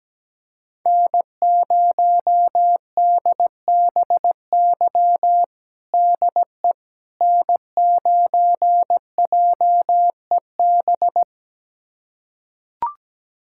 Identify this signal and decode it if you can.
Morse code, heard as a keyed tone, about 17 wpm, 700 Hz. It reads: N0DBY DE N9JEB
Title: USA Callsign DE Callsign 17wpm